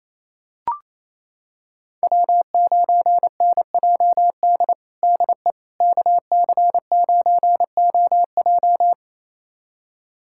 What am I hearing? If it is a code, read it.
Morse code, heard as a keyed tone, about 28 wpm, 700 Hz. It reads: W9NJD DE KC9OJ
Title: USA Callsign DE Callsign - 5 Characters - 28wpm